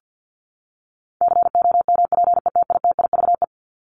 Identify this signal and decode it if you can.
Morse code, heard as a keyed tone, about 48 wpm, 700 Hz. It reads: COMPETITIVE
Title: Top 1001-1250 Words 50wpm